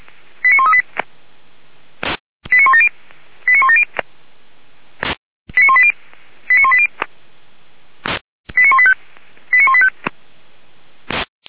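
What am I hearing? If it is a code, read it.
Five-tone selective call (ZVEI), heard as quick runs of five tones, tones of about 70 ms. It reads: 89138 89170 89170 89180 89180 89185 89185
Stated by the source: ZVEI five-tone call received by SDR, from the Orlamünde fire brigade (feuerwehr-orlamuende) server.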